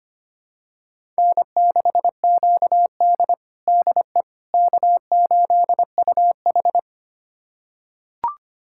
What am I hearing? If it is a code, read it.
Morse code, heard as a keyed tone, about 25 wpm, 700 Hz. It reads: N6QD DE K8UH
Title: USA Callsign DE Callsign - 4 Characters - 25wpm